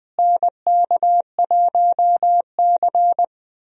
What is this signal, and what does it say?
Morse code, heard as a keyed tone, about 20 wpm, 700 Hz. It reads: NK1C
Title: USA Call Signs - 4 Characters - Repeated 2x - 20wpm